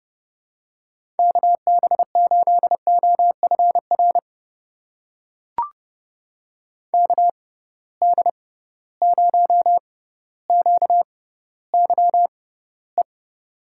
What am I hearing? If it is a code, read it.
Morse code, heard as a keyed tone, about 32 wpm, spaced as for 16 wpm, 700 Hz. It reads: KB8OFR KD0QYE
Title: USA Call Signs - 6 Characters - 6x Character Spacing - 30wpm